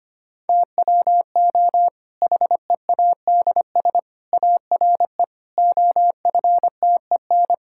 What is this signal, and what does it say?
Morse code, heard as a keyed tone, about 25 wpm, 700 Hz. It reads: TWO HEADS ARE OFTEN
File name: practice / 25wpm / Difficult Sentences with Hints - 25wpm